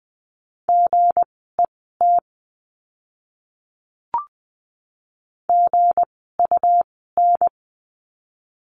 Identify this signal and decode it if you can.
Morse code, heard as a keyed tone, about 20 wpm, spaced as for 15 wpm, 700 Hz. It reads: GET GUN
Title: 160 Lesson 26f - Letter g - Words 20wpm